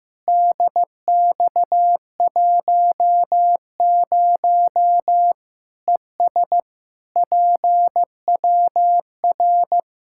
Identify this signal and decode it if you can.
Morse code, heard as a keyed tone, about 15 wpm, 700 Hz. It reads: DX10 ES PWR
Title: Practice QSO - Exchange 2 - Info - 15wpm